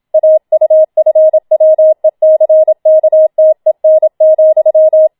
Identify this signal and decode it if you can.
Morse code, heard as a keyed tone, about 27 wpm, 630 Hz. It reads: AUFWECKTEN,